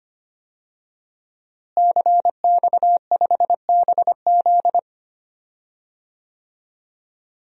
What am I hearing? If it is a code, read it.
Morse code, heard as a keyed tone, about 25 wpm, 700 Hz. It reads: CX5BZ
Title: DXCC-100 Call Signs - 5 Characters - Repeated 2x - 25wpm